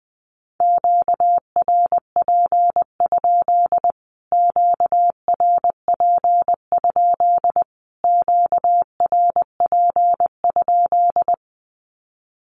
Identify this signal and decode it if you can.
Morse code, heard as a keyed tone, about 20 wpm, 700 Hz. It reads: QRP? QRP? QRP?
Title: Q-Codes Repeated 3x 20wpm